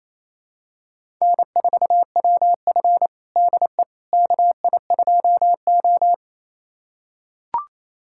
Morse code, heard as a keyed tone, about 28 wpm, 700 Hz. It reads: N4WF DE KI2O
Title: USA Callsign DE Callsign - 4 Characters - 28wpm